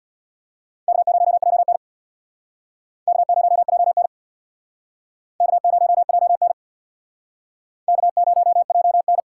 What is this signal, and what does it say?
Morse code, heard as a keyed tone, about 64 wpm, 700 Hz. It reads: K0JN K0JN K0JN K0JN